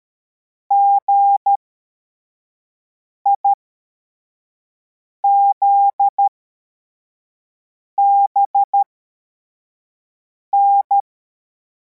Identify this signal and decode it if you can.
Morse code, heard as a keyed tone, about 13 wpm, spaced as for 4 wpm, 800 Hz. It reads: GIZBN
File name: formation / cw / Text42